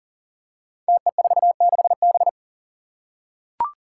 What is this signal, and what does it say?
Morse code, heard as a keyed tone, about 40 wpm, 700 Hz. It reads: TE46B
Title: Five Challenging Letter-Number Combinations 40wpm